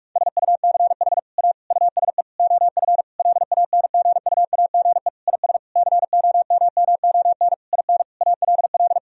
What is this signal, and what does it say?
Morse code, heard as a keyed tone, about 46 wpm, 700 Hz. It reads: SUCH A USE OF LANGUAGE IS COMMON IN ALL
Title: Sentences from Top 1000 Words - Review Part 3 - 45wpm